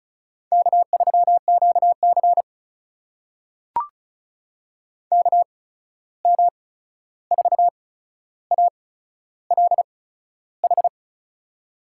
Morse code, heard as a keyed tone, about 36 wpm, spaced as for 13 wpm, 700 Hz. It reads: K3QC KM4ALH
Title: USA Call Signs - 8x Character Spacing - 35wpm